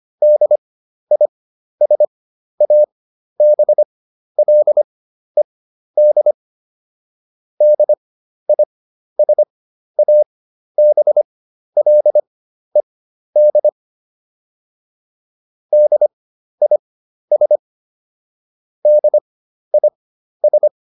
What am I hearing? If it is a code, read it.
Morse code, heard as a keyed tone, about 25 wpm, spaced as for 12 wpm, 600 Hz. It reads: DISABLED DISABLED DIS DIS